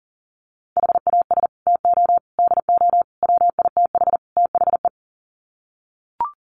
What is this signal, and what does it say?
Morse code, heard as a keyed tone, about 40 wpm, 700 Hz. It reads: HAS TO DO WITH THE
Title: Sets of 5 Words 40wpm